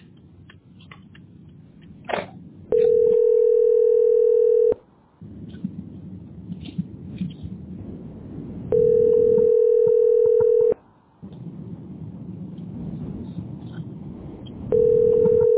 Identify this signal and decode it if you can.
Ringback tone (North America), 440 + 480 Hz.
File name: 46ebeeb9-6cc8-4019-81e4